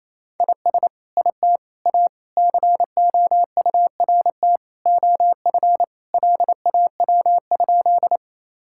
Morse code, heard as a keyed tone, about 28 wpm, 700 Hz. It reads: IS IT A COURT OF LAW?
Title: Sentences from Top 1600 Words 28wpm